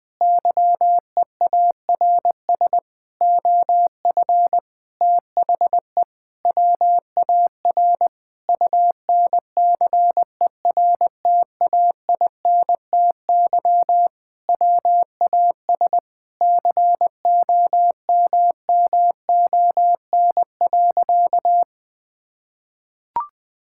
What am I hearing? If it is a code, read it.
Morse code, heard as a keyed tone, about 21 wpm, 700 Hz. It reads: YEARS OF THE WAR UNCERTAINTY WAS COMMON.
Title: Sentences with 6-Word Phrases - Spoken Beginning - 20wpm